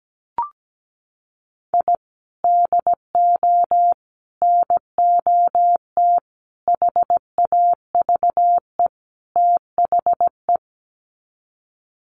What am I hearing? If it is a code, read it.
Morse code, heard as a keyed tone, about 17 wpm, 700 Hz. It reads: I DO NOT HAVE THE
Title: Sets of 5 Words 17wpm